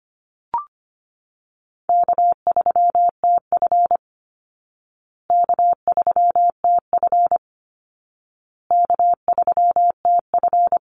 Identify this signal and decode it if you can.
Morse code, heard as a keyed tone, about 25 wpm, 700 Hz. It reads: K3TF K3TF K3TF
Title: Easy Call Signs Repeated 3x - 25wpm